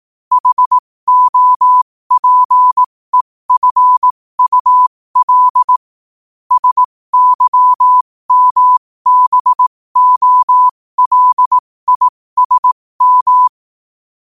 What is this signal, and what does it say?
Morse code, heard as a keyed tone, about 18 wpm, spaced as for 15 wpm, 1000 Hz. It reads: HOPEFUL SYMBOLISM
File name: qotdfiles / 15wpm